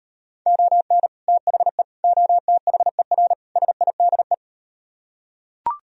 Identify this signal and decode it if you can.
Morse code, heard as a keyed tone, about 38 wpm, 700 Hz. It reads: ON THE OTHER SIDE